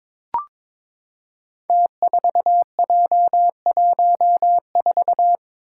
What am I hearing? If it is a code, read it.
Morse code, heard as a keyed tone, about 22 wpm, 700 Hz. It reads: T4J14
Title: Five Challenging Letter-Number Combinations 22wpm